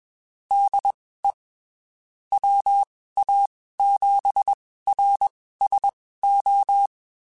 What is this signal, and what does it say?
Morse code, heard as a keyed tone, about 21 wpm, spaced as for 15 wpm, 800 Hz. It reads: DE WA7RSO
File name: (spaced) / QSOs Spaced-mp3s / WA7QOX 1 @20 sp